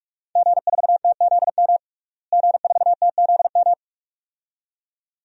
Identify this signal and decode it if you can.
Morse code, heard as a keyed tone, about 43 wpm, 700 Hz. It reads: G4TZM G4TZM
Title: DXCC-100 Call Signs - 5 Characters - Repeated 2x - 45wpm